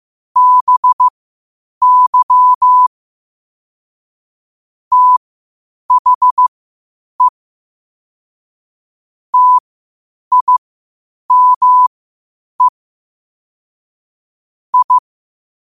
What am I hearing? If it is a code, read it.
Morse code, heard as a keyed tone, about 15 wpm, spaced as for 8 wpm, 1000 Hz. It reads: BY THE TIME I
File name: qotdfiles / 8wpm